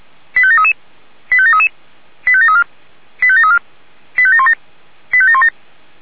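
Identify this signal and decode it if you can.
Five-tone selective call (ZVEI), heard as quick runs of five tones, tones of about 70 ms. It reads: 85733 85733 85734 85734 85717 85717
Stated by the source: ZVEI five-tone selective call 85942, received by SDR.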